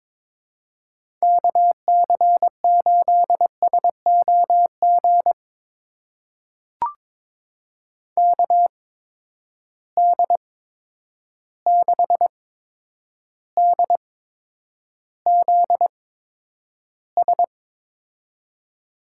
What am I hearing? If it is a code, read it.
Morse code, heard as a keyed tone, about 22 wpm, spaced as for 9 wpm, 700 Hz. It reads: KC8SOG KD6DZS
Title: USA Call Signs - 6 Characters - 8x Character Spacing - 22wpm